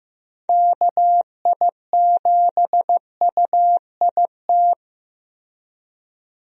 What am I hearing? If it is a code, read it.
Morse code, heard as a keyed tone, about 15 wpm, 700 Hz. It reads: KI7UIT
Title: USA Call Signs - 6 Characters - 15wpm